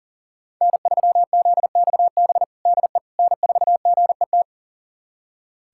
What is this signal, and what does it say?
Morse code, heard as a keyed tone, about 40 wpm, 700 Hz. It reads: N3ZXB DE N4GET
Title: USA Callsign DE Callsign - 5 Characters - 40wpm